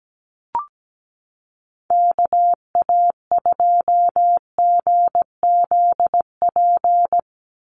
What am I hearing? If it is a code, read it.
Morse code, heard as a keyed tone, about 17 wpm, 700 Hz. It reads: KA2GZP
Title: USA Call Signs 17wpm